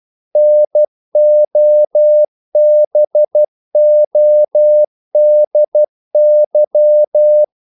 Morse code, heard as a keyed tone, about 12 wpm, 600 Hz. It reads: NOBODY
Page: English quotations in CW (telegraphy - morse)